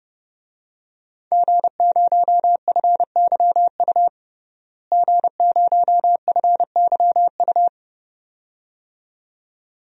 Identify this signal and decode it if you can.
Morse code, heard as a keyed tone, about 30 wpm, 700 Hz. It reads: G0FYU G0FYU
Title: DXCC-100 Call Signs - 5 Characters - Repeated 2x - 30wpm